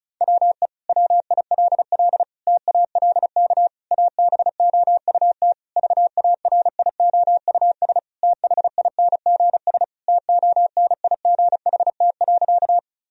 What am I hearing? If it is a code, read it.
Morse code, heard as a keyed tone, about 36 wpm, 700 Hz. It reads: WE WILL TALK ABOUT VARIOUS THINGS TONIGHT.
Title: Sentences from Top 1000 Words - Review Part 4 - 35wpm